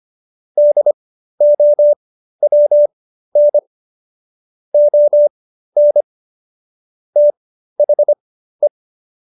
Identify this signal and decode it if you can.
Morse code, heard as a keyed tone, about 25 wpm, spaced as for 13 wpm, 590 Hz. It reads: DOWN ON THE